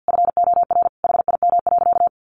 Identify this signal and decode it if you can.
Morse code, heard as a keyed tone, about 48 wpm, 700 Hz. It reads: FOR HIM.
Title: Sentences from Top 1300 Words 50wpm